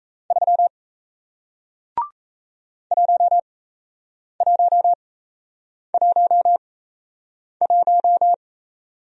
Morse code, heard as a keyed tone, about 36 wpm, spaced as for 8 wpm, 700 Hz. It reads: 3 1111